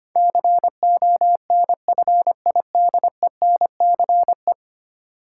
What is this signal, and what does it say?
Morse code, heard as a keyed tone, about 25 wpm, 700 Hz. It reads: CONFIDENCE
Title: Sentences from Top 1500 Words - New Word Drill - 25wpm